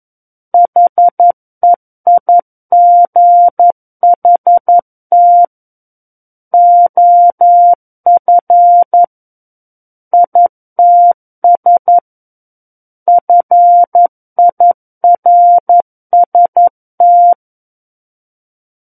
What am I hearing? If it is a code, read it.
Morse code, heard as a keyed tone, about 11 wpm, 700 Hz. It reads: HEIGHT OF ITS FIRST
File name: Book One / 02-Chapter 2 - 11 WPM